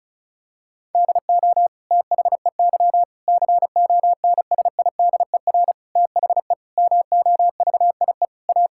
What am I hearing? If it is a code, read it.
Morse code, heard as a keyed tone, about 36 wpm, 700 Hz. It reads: DO THEY CONSIDER THE MOVIE A
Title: Sentences from Top 1000 Words - Review Part 6 - 35wpm